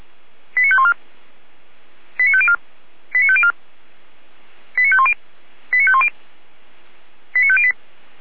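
Five-tone selective call (ZVEI), heard as quick runs of five tones, tones of about 70 ms. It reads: 89514 89504 89504 89510 89510 89508
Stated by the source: ZVEI five-tone selective call 89503 received on SDR.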